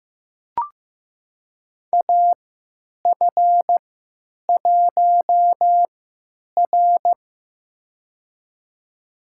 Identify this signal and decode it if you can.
Morse code, heard as a keyed tone, about 15 wpm, spaced as for 9 wpm, 700 Hz. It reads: AF1R
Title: USA Call Signs - 4 Characters - 3x Character Spacing - 15wpm